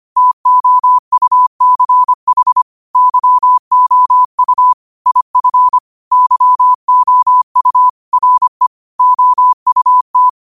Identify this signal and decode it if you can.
Morse code, heard as a keyed tone, about 25 wpm, 1000 Hz. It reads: TOUCH YOU IF YOU RE OUT
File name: qotdfiles / 25wpm